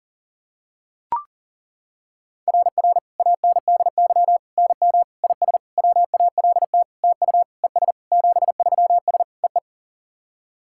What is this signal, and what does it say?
Morse code, heard as a keyed tone, about 40 wpm, 700 Hz. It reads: RR ANDY NM IS WALT TU ES 73S EE
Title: SOTA QSO Practice 40wpm